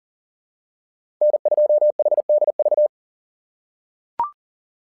Morse code, heard as a keyed tone, about 40 wpm, 600 Hz. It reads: N2HDV
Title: SOTA QSO Practice 40wpm